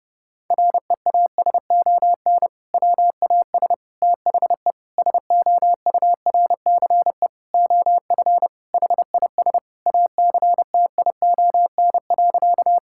Morse code, heard as a keyed tone, about 30 wpm, 700 Hz. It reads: REASON WAS THE SOURCE OF HIS ACTION.
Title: Sentences from Top 1000 Words - Review Part 5 - 30wpm